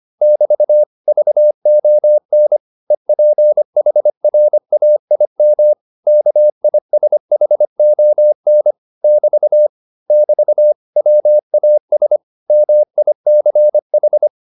Morse code, heard as a keyed tone, about 25 wpm, 600 Hz. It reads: <BT> VON EPHRAIM KISHON <BT> <BT> WAS MICH